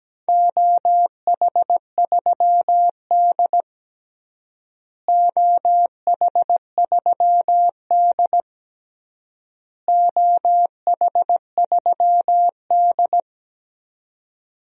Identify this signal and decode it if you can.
Morse code, heard as a keyed tone, about 17 wpm, 700 Hz. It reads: OH3D OH3D OH3D
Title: DXCC-100 Call Signs - 4 Characters - Repeated 3x - 17wpm